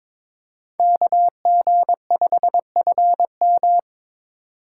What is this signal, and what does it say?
Morse code, heard as a keyed tone, about 25 wpm, 700 Hz. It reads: KG5FM
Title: Speed Racing - USA 2x2 Call Signs 22wpm